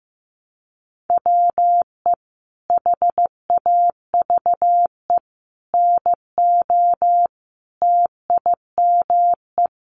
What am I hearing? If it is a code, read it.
Morse code, heard as a keyed tone, about 15 wpm, 700 Hz. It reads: WE HAVE NO TIME
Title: Sentences from Top 1100 Words 15wpm